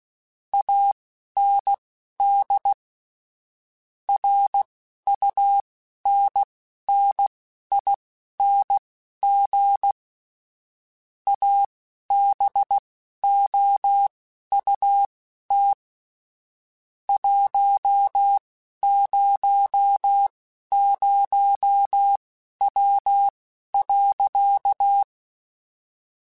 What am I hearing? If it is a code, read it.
Morse code, heard as a keyed tone, about 16 wpm, spaced as for 11 wpm, 800 Hz. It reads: AND RUNNING ABOUT 100W.
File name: QSOs / QSOs-mp3s / WD0FBT 1@10